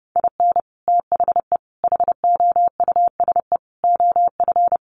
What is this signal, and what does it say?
Morse code, heard as a keyed tone, about 30 wpm, 700 Hz. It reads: IN THE HOUSE OF
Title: Sets of 5 Words Encore 30wpm